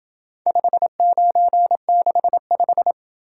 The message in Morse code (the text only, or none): 5965